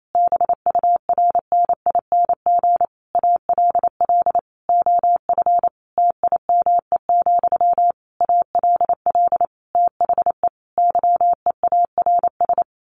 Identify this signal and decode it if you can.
Morse code, heard as a keyed tone, about 28 wpm, 700 Hz. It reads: BURNING ALL OF TIME, ALL THE YEARS
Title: A Sound Of Thunder 28wpm